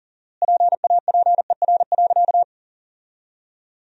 Morse code, heard as a keyed tone, about 40 wpm, 700 Hz. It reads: PAPER.
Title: 4x Word-Spacing - Sentences from Top 500 Words - 40wpm